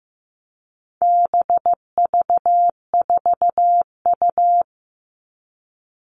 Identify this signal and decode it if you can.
Morse code, heard as a keyed tone, about 15 wpm, 700 Hz. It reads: BV4U
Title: All Cut Numbers - 4-Digit 15wpm